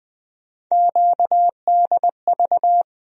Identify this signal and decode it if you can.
Morse code, heard as a keyed tone, about 20 wpm, 700 Hz. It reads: QDV
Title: USA Call Sign Postfixes - 3 Letters 20wpm